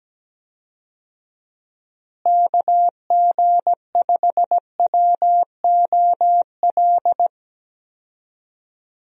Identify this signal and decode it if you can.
Morse code, heard as a keyed tone, about 17 wpm, 700 Hz. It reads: KG5WOL
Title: USA Call Signs - 6 Characters - Repeated 3x - 17wpm